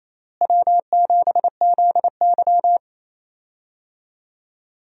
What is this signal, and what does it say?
Morse code, heard as a keyed tone, about 28 wpm, 700 Hz. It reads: W7ZY
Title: USA Call Signs - 4 Characters - Repeated 3x - 28wpm